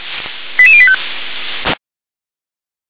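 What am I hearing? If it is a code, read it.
Five-tone selective call (ZVEI), heard as quick runs of five tones, tones of about 70 ms. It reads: 88085
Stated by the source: ZVEI five-tone call 88170 received by SDR.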